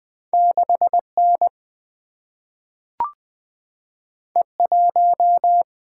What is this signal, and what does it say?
Morse code, heard as a keyed tone, about 20 wpm, 700 Hz. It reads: 6N E1
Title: Single-Letter Single-Number 20wpm